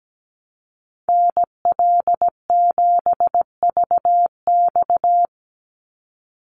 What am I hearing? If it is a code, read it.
Morse code, heard as a keyed tone, about 17 wpm, 700 Hz. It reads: NL7VX
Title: Call Signs - 17wpm